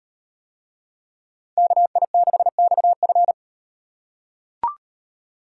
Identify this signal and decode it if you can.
Morse code, heard as a keyed tone, about 38 wpm, 700 Hz. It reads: KI6XF